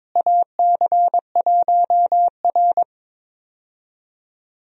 Morse code, heard as a keyed tone, about 21 wpm, 700 Hz. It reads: AC1R
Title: USA Call Signs - 4 Characters - Repeated 3x - 22wpm